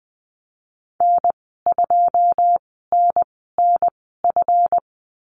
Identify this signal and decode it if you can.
Morse code, heard as a keyed tone, about 20 wpm, spaced as for 15 wpm, 700 Hz. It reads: N2NNF
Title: 161 Lesson 26g - Letter g - Call Signs 20wpm